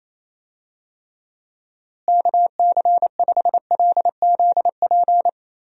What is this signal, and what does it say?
Morse code, heard as a keyed tone, about 28 wpm, 700 Hz. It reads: KC5LZP
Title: USA Call Signs - 6 Characters - Repeated 2x - 28wpm